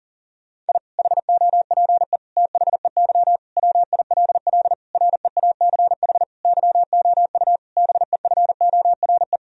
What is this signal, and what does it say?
Morse code, heard as a keyed tone, about 40 wpm, 700 Hz. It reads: I HOPE THEY WILL REACH YOU BEFORE
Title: Sentences from Top 1000 Words - Review Part 2 - 40wpm